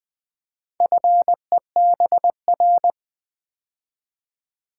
Morse code, heard as a keyed tone, about 20 wpm, 700 Hz. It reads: FEBR
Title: Build A Word - 4 Letters At A Time - 8-Letter Words - 20wpm